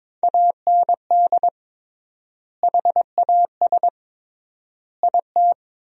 Morse code, heard as a keyed tone, about 21 wpm, 700 Hz. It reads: AND HAS IT
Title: 3x Word-Spacing - Sentences from Top 100 Words - 22wpm